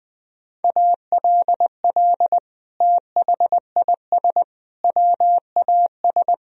Morse code, heard as a keyed tone, about 20 wpm, 700 Hz. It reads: ALL THIS WAS
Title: Sentences from Top 1000 Words - Review Part 4 - 20wpm